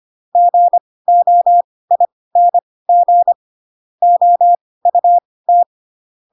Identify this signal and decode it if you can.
Morse code, heard as a keyed tone, about 25 wpm, spaced as for 18 wpm, 700 Hz. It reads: GOING OUT